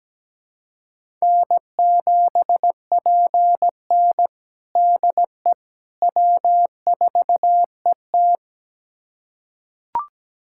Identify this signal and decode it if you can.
Morse code, heard as a keyed tone, about 17 wpm, 700 Hz. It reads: N7PN DE W4ET
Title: USA Callsign DE Callsign - 4 Characters - 17wpm